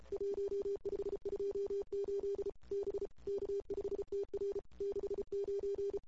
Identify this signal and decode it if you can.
Morse code, heard as a keyed tone, about 33 wpm, 400 Hz. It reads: 1528 B K5TR 69
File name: CW / 2008-ss-cw-k5tr-op-n5rz -02:00